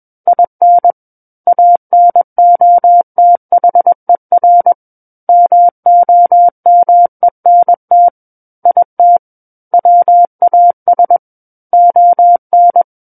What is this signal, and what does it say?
Morse code, heard as a keyed tone, about 21 wpm, 700 Hz. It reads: IN ANOTHER MOMENT IT WAS ON
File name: Book One / 12-Chapter 12 - 21 WPM